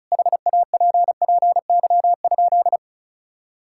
Morse code, heard as a keyed tone, about 36 wpm, 700 Hz. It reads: HAPPY?
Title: Sentences from Top 1000 Words - Review Part 6 - 35wpm